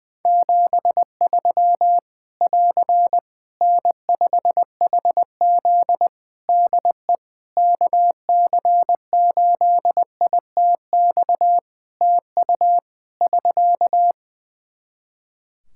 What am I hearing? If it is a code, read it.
Morse code, heard as a keyed tone, about 21 wpm, 700 Hz. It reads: 73 <AR> N5HZ DE KC8ITX TU <SK>
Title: Practice QSO - Exchange 4 - Ending - 20wpm